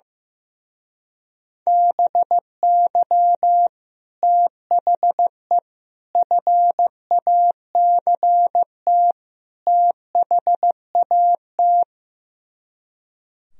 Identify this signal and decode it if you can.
Morse code, heard as a keyed tone, about 15 wpm, 700 Hz. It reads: BY THE FACT THAT
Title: Speed-Racing - Sets of 4 Words 15wpm